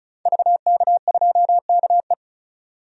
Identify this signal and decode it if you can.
Morse code, heard as a keyed tone, about 34 wpm, 700 Hz. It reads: VK2KE
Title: DXCC-100 Call Signs - 5 Characters - 35wpm